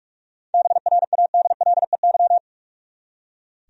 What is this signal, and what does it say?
Morse code, heard as a keyed tone, about 46 wpm, 700 Hz. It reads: BRADLEY
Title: USA Common Names - 45wpm